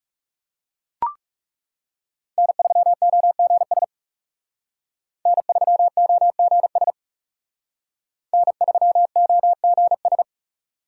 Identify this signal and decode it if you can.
Morse code, heard as a keyed tone, about 36 wpm, 700 Hz. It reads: N3OGS N3OGS N3OGS
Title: Speed Racing - USA 1x3 Call Signs 30wpm